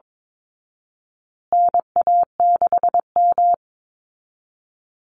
Morse code, heard as a keyed tone, about 22 wpm, 700 Hz. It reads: NA6M
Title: Call Signs Repeated 3x - 22wpm